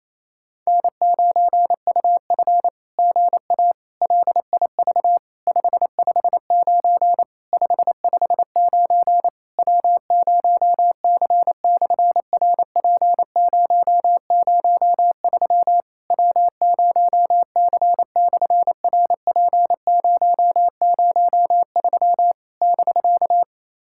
Morse code, heard as a keyed tone, about 28 wpm, 700 Hz. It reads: N9UF GA LIV 559 559 W0C/RP003 W0C/RP003 <BK>
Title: SOTA QSO Practice 28wpm